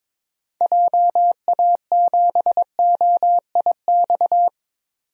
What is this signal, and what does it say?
Morse code, heard as a keyed tone, about 22 wpm, 700 Hz. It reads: JA7OIX
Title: DXCC-100 Call Signs - 6 Characters - 22wpm